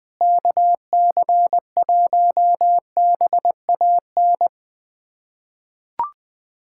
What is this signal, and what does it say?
Morse code, heard as a keyed tone, about 20 wpm, 700 Hz. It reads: KC1BAN
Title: USA Callsign DE Callsign - 6 Characters - 20wpm